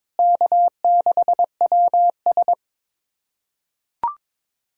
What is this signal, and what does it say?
Morse code, heard as a keyed tone, about 22 wpm, 700 Hz. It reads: K6WS
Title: USA Callsign DE Callsign - 4 Characters - 22wpm